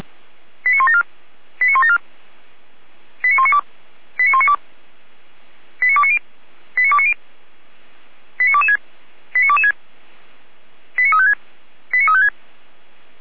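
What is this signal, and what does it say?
Five-tone selective call (ZVEI), heard as quick runs of five tones, tones of about 70 ms. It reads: 89174 89174 89192 89192 89290 89290 89226 89226 89356 89356
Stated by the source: ZVEI five-tone selective call 89500 received via SDR by feuerwehr-orlamuende.org (Feuerwehr Orlamünde fire brigade).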